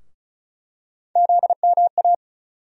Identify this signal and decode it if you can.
Morse code, heard as a keyed tone, about 36 wpm, 700 Hz. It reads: ZMA
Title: USA Call Sign Postfixes - 3 Letters 35wpm